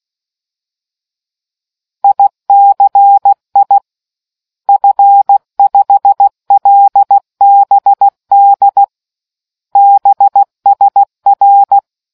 Morse code, heard as a keyed tone, about 16 wpm, 800 Hz. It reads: ICI F5LBD BSR